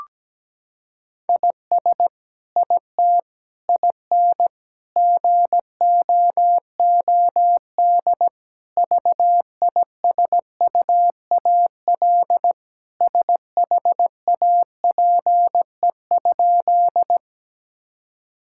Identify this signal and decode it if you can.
Morse code, heard as a keyed tone, about 17 wpm, 700 Hz. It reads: IS IT IN GOOD VISUAL SHAPE?
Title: Sentences from Top 1000 Words - Review Part 6 - 17wpm